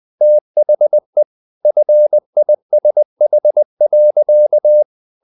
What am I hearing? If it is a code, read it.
Morse code, heard as a keyed tone, about 20 wpm, 600 Hz. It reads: THE FISH.
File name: the old man and the sea part 2 20wpm